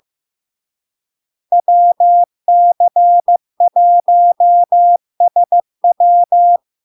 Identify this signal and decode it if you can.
Morse code, heard as a keyed tone, about 15 wpm, 700 Hz. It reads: WC1SW
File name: LARK-FD-2022-Log-in-CW-15wpm-700-Hz